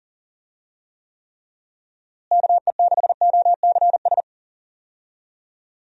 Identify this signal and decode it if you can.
Morse code, heard as a keyed tone, about 40 wpm, 700 Hz. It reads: KE6OCS
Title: USA Call Signs - 6 Characters - Repeated 3x - 40wpm